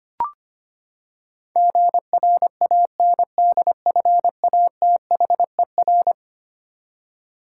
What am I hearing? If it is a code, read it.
Morse code, heard as a keyed tone, about 25 wpm, 700 Hz. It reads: GRANDFATHER
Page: morse-code-ninja-practice-25wpm